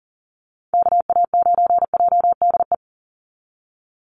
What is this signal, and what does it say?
Morse code, heard as a keyed tone, about 40 wpm, 700 Hz. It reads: KA9JDE
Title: USA Call Signs 40wpm